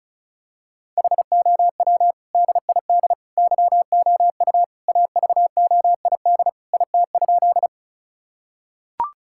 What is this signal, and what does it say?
Morse code, heard as a keyed tone, about 36 wpm, 700 Hz. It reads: HOW DID YOU AVOID IT?
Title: Sentences from Top 1000 Words - Review Part 6 - 35wpm